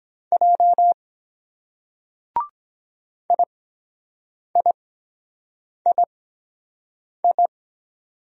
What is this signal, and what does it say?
Morse code, heard as a keyed tone, about 22 wpm, spaced as for 7 wpm, 700 Hz. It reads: J IIII